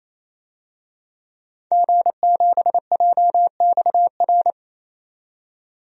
Morse code, heard as a keyed tone, about 28 wpm, 700 Hz. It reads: G7JXR
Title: DXCC-100 Call Signs - 5 Characters - 28wpm